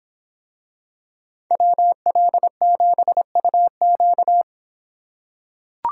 Morse code, heard as a keyed tone, about 24 wpm, 700 Hz. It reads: WL7UQ